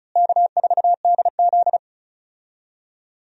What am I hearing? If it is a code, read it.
Morse code, heard as a keyed tone, about 36 wpm, 700 Hz. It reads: K4DZ
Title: USA Call Signs - 4 Characters - Repeated 2x - 35wpm